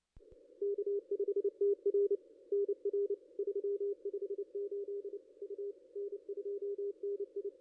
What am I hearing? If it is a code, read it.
Morse code, heard as a keyed tone, about 30 wpm, 410 Hz. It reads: K5TR NR 358 U N2NI
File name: contests / ss-cw-k5tr-op-n5rz -0500